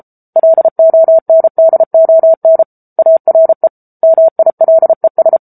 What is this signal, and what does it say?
Morse code, heard as a keyed tone, about 34 wpm, 650 Hz. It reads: LONDON ARE MILES